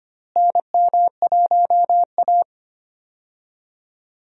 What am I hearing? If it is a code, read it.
Morse code, heard as a keyed tone, about 25 wpm, 700 Hz. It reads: NM1A
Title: USA Call Signs - 4 Characters - Repeated 2x - 25wpm